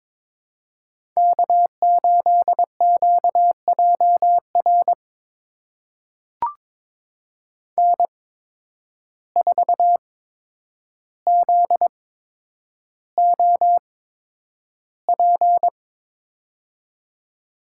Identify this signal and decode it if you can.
Morse code, heard as a keyed tone, about 22 wpm, spaced as for 9 wpm, 700 Hz. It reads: K8QJR N4ZOP
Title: USA Call Signs - 8x Character Spacing - 22wpm